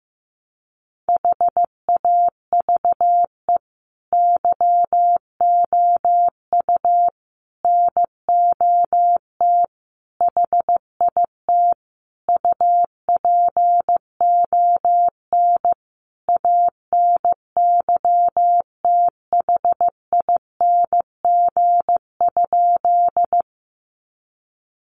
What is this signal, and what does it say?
Morse code, heard as a keyed tone, about 15 wpm, 700 Hz. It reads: HAVE YOU NOT HIT UPON ANYTHING?
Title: Sentences from Top 1200 Words 15wpm